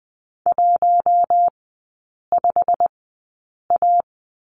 Morse code, heard as a keyed tone, about 20 wpm, spaced as for 8 wpm, 700 Hz. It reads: 15A
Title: 100 Lesson 17e - Letter u - Three Characters 20wpm